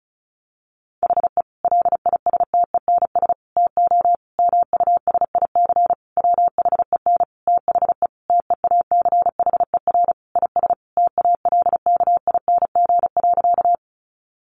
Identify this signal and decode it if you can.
Morse code, heard as a keyed tone, about 34 wpm, 700 Hz. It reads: HE LISTENS TO MUSIC WHEN THE TEACHER IS TALKING.
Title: Additional Easy Phrases 35wpm